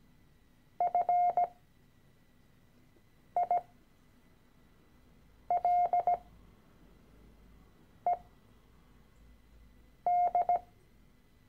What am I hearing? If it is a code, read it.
Morse code, heard as a keyed tone, about 17 wpm, spaced as for 4 wpm, 710 Hz. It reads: FILED
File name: CW-Online-Kurs / Anfänger / Lektion 013